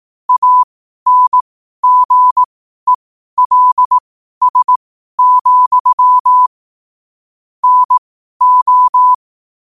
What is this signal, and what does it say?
Morse code, heard as a keyed tone, about 18 wpm, spaced as for 12 wpm, 1000 Hz. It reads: ANGELS, NO
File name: qotdfiles / 12wpm